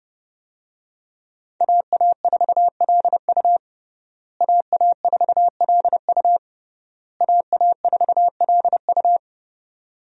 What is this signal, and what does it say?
Morse code, heard as a keyed tone, about 30 wpm, 700 Hz. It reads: AA4LU AA4LU AA4LU
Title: USA Call Signs - 5 Characters - Repeated 3x - 30wpm